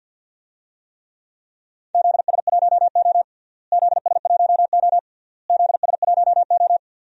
Morse code, heard as a keyed tone, about 49 wpm, 700 Hz. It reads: ZS1O ZS1O ZS1O
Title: DXCC-100 Call Signs - 4 Characters - Repeated 3x - 50wpm